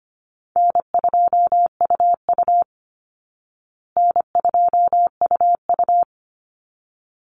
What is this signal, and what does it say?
Morse code, heard as a keyed tone, about 25 wpm, 700 Hz. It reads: N2UU N2UU
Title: Call Signs Repeated 3x - 25wpm